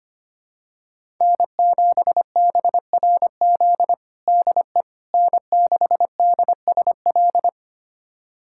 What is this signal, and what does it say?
Morse code, heard as a keyed tone, about 25 wpm, 700 Hz. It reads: N7BRZ DE N6DSL
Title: USA Callsign DE Callsign - 5 Characters - 25wpm